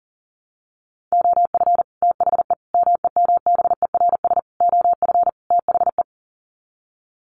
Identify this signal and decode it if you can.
Morse code, heard as a keyed tone, about 40 wpm, 700 Hz. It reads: OF THE MEMBERS OF THE
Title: Sets of 5 Words Encore2 40wpm